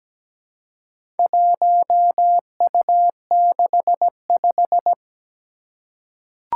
Morse code, heard as a keyed tone, about 17 wpm, 700 Hz. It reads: 1U65
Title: Four Challenging Letter-Number Combinations 17wpm